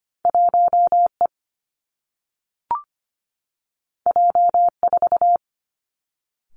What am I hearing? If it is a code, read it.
Morse code, heard as a keyed tone, about 25 wpm, 700 Hz. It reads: 1E J4
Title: Two Challenging Letter-Number Combinations 25wpm